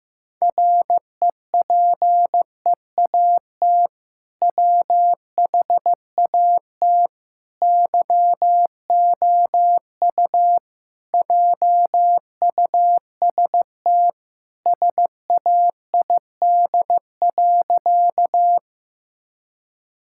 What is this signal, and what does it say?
Morse code, heard as a keyed tone, about 16 wpm, 700 Hz. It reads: REPEAT WHAT YOU JUST SAID.
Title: Sentences from Top 1600 Words 15wpm